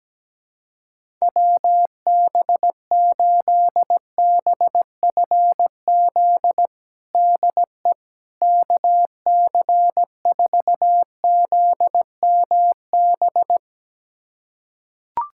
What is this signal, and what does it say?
Morse code, heard as a keyed tone, about 17 wpm, 700 Hz. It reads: WB8BFZ DE KC4ZMB
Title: USA Callsign DE Callsign - 6 Characters - 17wpm